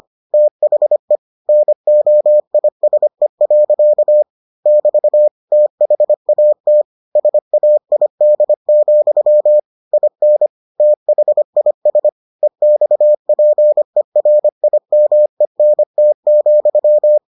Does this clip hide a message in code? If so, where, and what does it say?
Yes: Morse code from 0.3 to 17.2 s, heard as a keyed tone, about 25 wpm, 600 Hz. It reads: THE NOISE. <BT> THAT SAID, IN THIS EXPERIMENT,